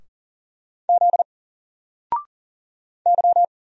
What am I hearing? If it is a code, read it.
Morse code, heard as a keyed tone, about 40 wpm, 700 Hz. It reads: Z Y
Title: Single Character (Letter:Number:Punctuation:Prosign) 40wpm